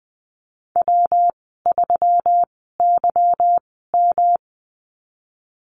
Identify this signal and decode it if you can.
Morse code, heard as a keyed tone, about 20 wpm, spaced as for 15 wpm, 700 Hz. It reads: W3YM
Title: 213 Lesson 34g - Number 8 - Call Signs 20wpm